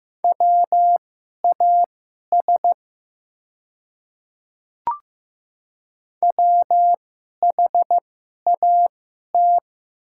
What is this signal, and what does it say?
Morse code, heard as a keyed tone, about 15 wpm, spaced as for 11 wpm, 700 Hz. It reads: WAS WHAT
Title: 2x Character Spacing - Top 100 Words - 15wpm